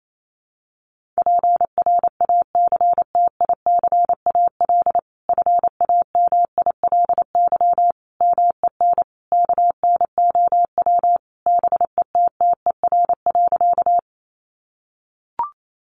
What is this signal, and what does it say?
Morse code, heard as a keyed tone, about 28 wpm, 700 Hz. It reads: PRACTICAL FAMILY MEN KNOW BETTER.
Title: Sentences from Top 1400 Words 28wpm